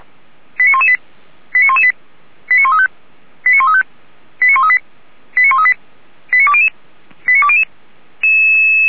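Five-tone selective call (ZVEI), heard as quick runs of five tones, tones of about 70 ms. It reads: 89118 89118 89135 89135 89138 89138 89200 89200
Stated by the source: ZVEI five-tone call 85730 received on an SDR.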